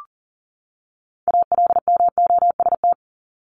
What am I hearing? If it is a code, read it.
Morse code, heard as a keyed tone, about 39 wpm, 700 Hz. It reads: ALMOST
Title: Top 501-750 Words 40wpm